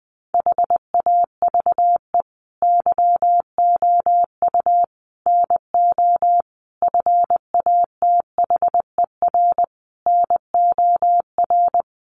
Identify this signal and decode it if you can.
Morse code, heard as a keyed tone, about 20 wpm, 700 Hz. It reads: HAVE YOU NO FATHER NOR
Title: Sentences from Top 1500 Words 20wpm